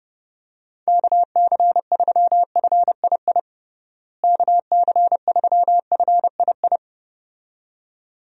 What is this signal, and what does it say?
Morse code, heard as a keyed tone, about 30 wpm, 700 Hz. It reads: KC3FII KC3FII
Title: USA Call Signs - 6 Characters - Repeated 2x - 30wpm